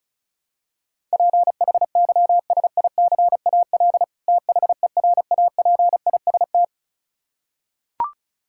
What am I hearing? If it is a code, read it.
Morse code, heard as a keyed tone, about 36 wpm, 700 Hz. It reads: PHYSICAL THERAPIST
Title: Practice QSO - Exchange 3 - More Info - 35wpm